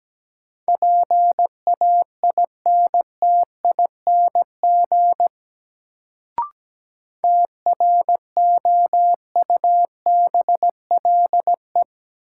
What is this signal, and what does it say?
Morse code, heard as a keyed tone, about 17 wpm, 700 Hz. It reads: PAINTING TROUBLE
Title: Sentences from Top 1200 Words - New Word Drill Alt - 17wpm